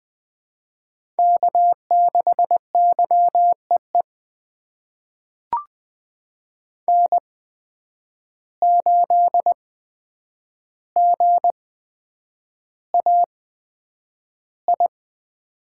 Morse code, heard as a keyed tone, about 20 wpm, spaced as for 9 wpm, 700 Hz. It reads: K6YEE N8GAI
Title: USA Call Signs - 5 Characters - 8x Character Spacing - 20wpm